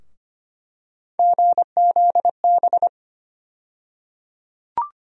Morse code, heard as a keyed tone, about 25 wpm, 700 Hz. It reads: GZB
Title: USA Call Sign Postfixes - 3 Letters 25wpm